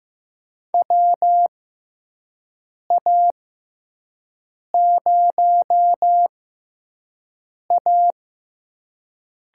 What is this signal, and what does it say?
Morse code, heard as a keyed tone, about 15 wpm, spaced as for 5 wpm, 700 Hz. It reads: WA0A
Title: USA Call Signs - 5 Characters - 6x Character Spacing - 15wpm